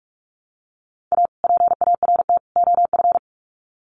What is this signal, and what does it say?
Morse code, heard as a keyed tone, about 44 wpm, 700 Hz. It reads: A PART OF
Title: Sets of 3 Words 45wpm